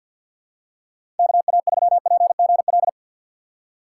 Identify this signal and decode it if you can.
Morse code, heard as a keyed tone, about 53 wpm, 700 Hz. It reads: KA3PGL
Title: USA Call Signs - 6 Characters - 8x Character Spacing - 50wpm